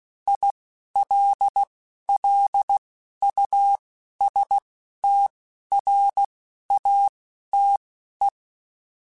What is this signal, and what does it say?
Morse code, heard as a keyed tone, about 16 wpm, spaced as for 11 wpm, 800 Hz. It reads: ILLUSTRATE
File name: Technical Subjects / Technical Subjects-mp3s / AND Gate @10